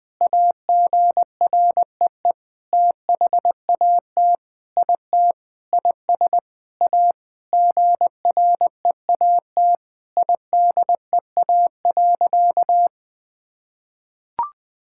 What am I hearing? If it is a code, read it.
Morse code, heard as a keyed tone, about 20 wpm, 700 Hz. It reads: AGREE THAT IT IS A GREAT IDEA.
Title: Sentences with 7-Word Phrases - 20wpm